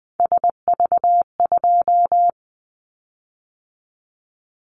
Morse code, heard as a keyed tone, about 20 wpm, 700 Hz. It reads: SV2
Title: Callsign Prefixes for Top 50 DXCC - 20wpm